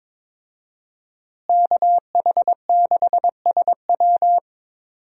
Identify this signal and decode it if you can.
Morse code, heard as a keyed tone, about 22 wpm, 700 Hz. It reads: KH6SW
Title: DXCC-100 Call Signs - 5 Characters - Repeated 2x - 22wpm